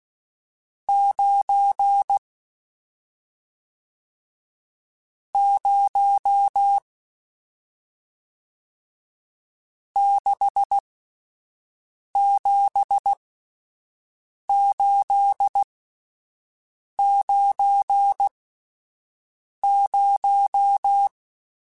Morse code, heard as a keyed tone, about 16 wpm, spaced as for 5 wpm, 800 Hz. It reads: 9 0 67890